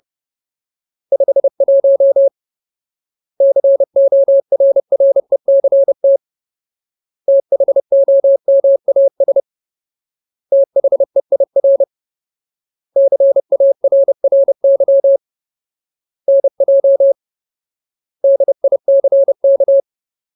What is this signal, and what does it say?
Morse code, heard as a keyed tone, about 30 wpm, 550 Hz. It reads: 51 CORRECT THOMAS THEIR CARRY NJ DICK
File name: Ultimate Word List / 30WPM Ultimate